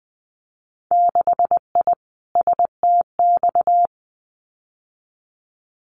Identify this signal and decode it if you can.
Morse code, heard as a keyed tone, about 20 wpm, 700 Hz. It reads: 6I STX
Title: Winter Field Day Exchanges Encore 20wpm